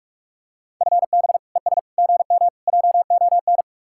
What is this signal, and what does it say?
Morse code, heard as a keyed tone, about 44 wpm, 700 Hz. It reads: FB ES GM JON